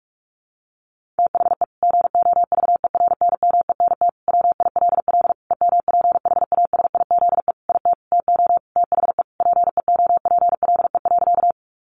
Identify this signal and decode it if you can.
Morse code, heard as a keyed tone, about 45 wpm, 700 Hz. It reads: THE GOVERNMENT WILL EMPHASIZE IT TO THE PEOPLE.
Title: Sentences from Top 1500 Words 45wpm